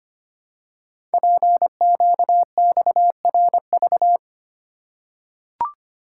Morse code, heard as a keyed tone, about 25 wpm, 700 Hz. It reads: PQXRV
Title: Code Groups - 5 Letters - 25wpm